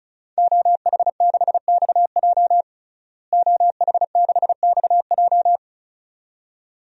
Morse code, heard as a keyed tone, about 36 wpm, 700 Hz. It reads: OH6XJ OH6XJ
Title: DXCC-100 Call Signs - 5 Characters - Repeated 2x - 35wpm